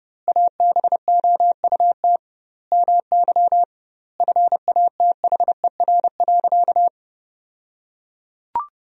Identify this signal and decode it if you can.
Morse code, heard as a keyed tone, about 30 wpm, 700 Hz. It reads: ABOUT MY FATHER.
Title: 2x Word-Spacing - Sentences from Top 400 Words - 30wpm